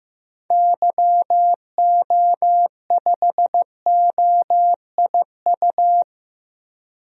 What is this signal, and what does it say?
Morse code, heard as a keyed tone, about 15 wpm, 700 Hz. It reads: YO5OIU
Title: DXCC-100 Call Signs - 6 Characters - Repeated 3x - 15wpm